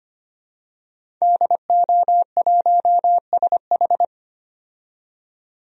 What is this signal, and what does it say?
Morse code, heard as a keyed tone, about 25 wpm, 700 Hz. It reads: DO1SH
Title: DXCC-100 Call Signs - 5 Characters - 25wpm